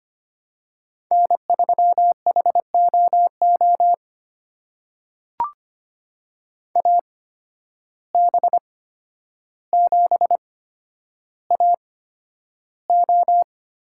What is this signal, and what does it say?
Morse code, heard as a keyed tone, about 25 wpm, spaced as for 11 wpm, 700 Hz. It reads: N3HOO AB7AO
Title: USA Call Signs - 5 Characters - 8x Character Spacing - 25wpm